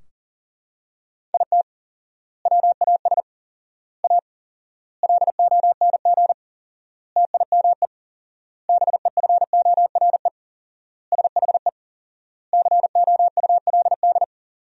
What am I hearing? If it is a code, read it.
Morse code, heard as a keyed tone, about 40 wpm, 700 Hz. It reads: IT WAS A LONG TIME BEFORE SHE COULD
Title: 4x Word-Spacing - Sentences from Top 500 Words - 40wpm